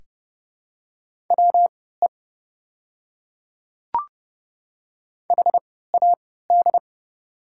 Morse code, heard as a keyed tone, about 30 wpm, spaced as for 17 wpm, 700 Hz. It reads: WE HAD
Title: 3x Character Spacing - Top 300 Words - 30wpm